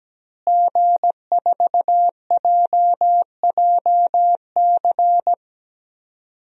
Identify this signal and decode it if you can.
Morse code, heard as a keyed tone, about 17 wpm, 700 Hz. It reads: G4JJC
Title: DXCC-100 Call Signs - 5 Characters - Repeated 2x - 17wpm